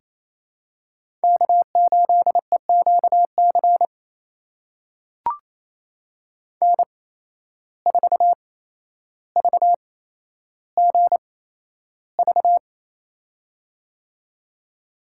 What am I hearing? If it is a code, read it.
Morse code, heard as a keyed tone, about 28 wpm, spaced as for 12 wpm, 700 Hz. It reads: K8EQC N4VGV
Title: USA Call Signs - 5 Characters - 8x Character Spacing - 28wpm